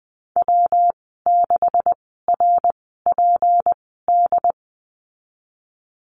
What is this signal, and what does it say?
Morse code, heard as a keyed tone, about 20 wpm, spaced as for 14 wpm, 700 Hz. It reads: W6RPD
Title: 154 Lesson 25g - Letter p - Call Signs 20wpm